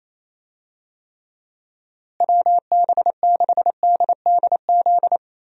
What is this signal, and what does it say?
Morse code, heard as a keyed tone, about 28 wpm, 700 Hz. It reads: WB6DDZ
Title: USA Call Signs - 6 Characters - Repeated 3x - 28wpm